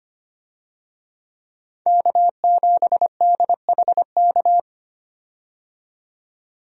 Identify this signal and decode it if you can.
Morse code, heard as a keyed tone, about 25 wpm, 700 Hz. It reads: K7DHK
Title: USA Call Signs - 5 Characters - 25wpm